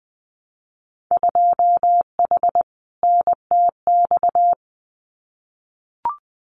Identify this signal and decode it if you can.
Morse code, heard as a keyed tone, about 20 wpm, 700 Hz. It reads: 2H NTX
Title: Winter Field Day Exchanges Encore 20wpm